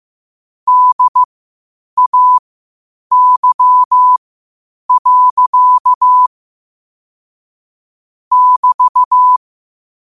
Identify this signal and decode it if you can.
Morse code, heard as a keyed tone, about 15 wpm, spaced as for 8 wpm, 1000 Hz. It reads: DAY. <BT>
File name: qotdfiles / 8wpm